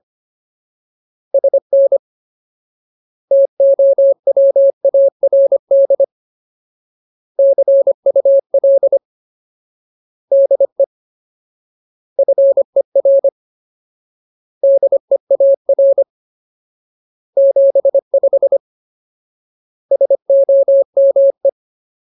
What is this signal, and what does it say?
Morse code, heard as a keyed tone, about 25 wpm, 550 Hz. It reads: SN TOWARD CUL DE FER DEAR 75 SOME
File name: Ultimate Word List / 25WPM Ultimate